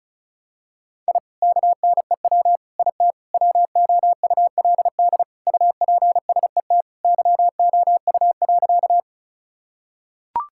Morse code, heard as a keyed tone, about 36 wpm, 700 Hz. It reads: I KNEW IT WOULD UPSET YOU.
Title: Sentences from Top 1000 Words - Review Part 5 - 35wpm